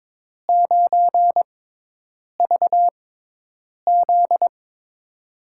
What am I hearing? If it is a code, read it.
Morse code, heard as a keyed tone, about 22 wpm, spaced as for 8 wpm, 700 Hz. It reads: 9VZ
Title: USA Call Signs - 5 Characters - 6x Character Spacing - 22wpm